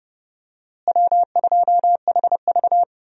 Morse code, heard as a keyed tone, about 30 wpm, 700 Hz. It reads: W2HV
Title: USA Call Signs - 4 Characters - Repeated 3x - 30wpm